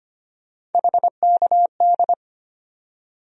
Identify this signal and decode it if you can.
Morse code, heard as a keyed tone, about 25 wpm, 700 Hz. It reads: HKD
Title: USA Call Sign Postfixes - 3 Letters 25wpm